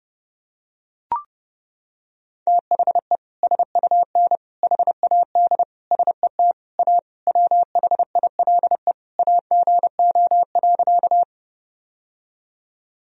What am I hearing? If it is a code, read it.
Morse code, heard as a keyed tone, about 30 wpm, 700 Hz. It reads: THE SUN HAD SET A WHILE AGO.
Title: Sentences from Top 1000 Words - Review Part 1 - 30wpm